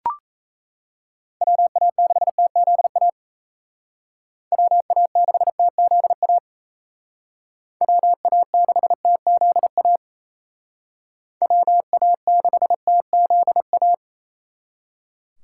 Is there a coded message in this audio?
Yes: Morse code from 1.4 to 13.9 s, heard as a keyed tone, about 35 wpm, 700 Hz. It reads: WA6TZA WA6TZA WA6TZA WA6TZA